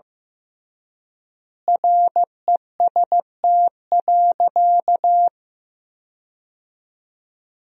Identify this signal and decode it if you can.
Morse code, heard as a keyed tone, about 15 wpm, 700 Hz. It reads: REST.
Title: 3x Word-Spacing - Sentences from Top 500 Words - 15wpm